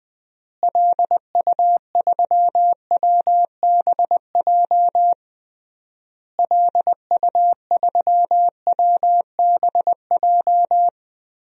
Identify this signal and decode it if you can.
Morse code, heard as a keyed tone, about 21 wpm, 700 Hz. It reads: LU3WBJ LU3WBJ
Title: DXCC-100 Call Signs - 6 Characters - Repeated 3x - 20wpm